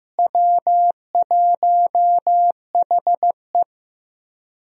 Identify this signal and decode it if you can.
Morse code, heard as a keyed tone, about 15 wpm, 700 Hz. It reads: W1HE
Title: USA Call Signs - 4 Characters - Repeated 3x - 15wpm